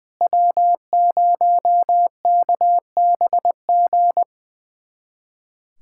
Morse code, heard as a keyed tone, about 20 wpm, 700 Hz. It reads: W0KBG
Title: USA Callsign DE Callsign - 5 Characters - 20wpm